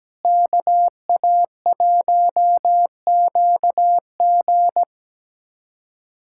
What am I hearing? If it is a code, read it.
Morse code, heard as a keyed tone, about 17 wpm, 700 Hz. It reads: KA1QG
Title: USA Callsign DE Callsign - 5 Characters - 17wpm